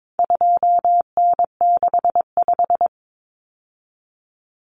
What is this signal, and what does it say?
Morse code, heard as a keyed tone, about 22 wpm, 700 Hz. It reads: 2N65
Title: Common Cut Numbers - 4-Digit 22wpm